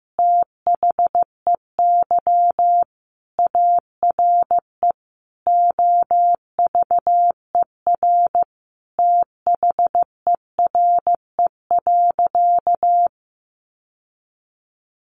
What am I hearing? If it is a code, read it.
Morse code, heard as a keyed tone, about 15 wpm, 700 Hz. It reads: THEY ARE OVER THERE.
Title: Sentences from Top 600 Words 15wpm